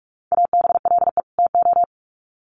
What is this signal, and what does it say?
Morse code, heard as a keyed tone, about 44 wpm, 700 Hz. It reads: ABLE TO
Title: Sets of 2 Words 45wpm